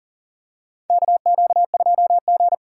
Morse code, heard as a keyed tone, about 40 wpm, 700 Hz. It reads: KQ2G
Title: USA Call Signs - 4 Characters - 3x Character Spacing - 40wpm